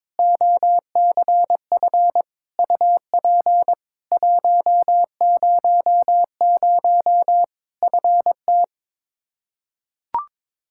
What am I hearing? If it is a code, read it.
Morse code, heard as a keyed tone, about 21 wpm, 700 Hz. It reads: OCF UP 100 FT
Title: K1USN SST QSO Practice 22wpm